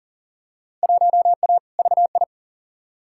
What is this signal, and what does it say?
Morse code, heard as a keyed tone, about 38 wpm, 700 Hz. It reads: 1A VI
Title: Field Day Exchanges Encore - 40wpm